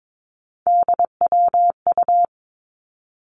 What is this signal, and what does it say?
Morse code, heard as a keyed tone, about 22 wpm, 700 Hz. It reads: DWU
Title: Three Challenging Letter-Number Combinations 22wpm